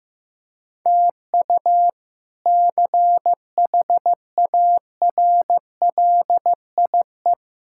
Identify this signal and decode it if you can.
Morse code, heard as a keyed tone, about 15 wpm, 700 Hz. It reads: TU CHARLIE
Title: K1USN SST QSO Practice - 15wpm